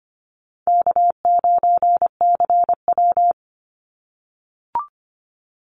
Morse code, heard as a keyed tone, about 25 wpm, 700 Hz. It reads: K9CW
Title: Easy Call Signs - 25wpm